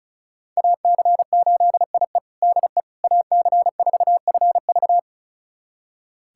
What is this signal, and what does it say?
Morse code, heard as a keyed tone, about 34 wpm, 690 Hz. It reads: AC8IE DE AC4FV
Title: USA Callsign DE Callsign - 5 Characters - 35wpm